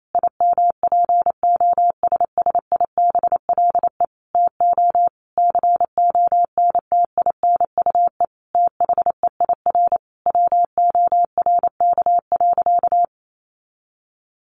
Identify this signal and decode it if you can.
Morse code, heard as a keyed tone, about 28 wpm, 700 Hz. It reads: IMPOSSIBLE TO CONTINUE THEIR WORK.
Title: Sentences from Top 1100 Words 28wpm